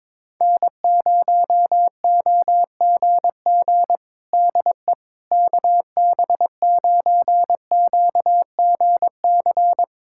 Morse code, heard as a keyed tone, about 22 wpm, 700 Hz. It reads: N0OGG DE KB9QGC
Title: USA Callsign DE Callsign 22wpm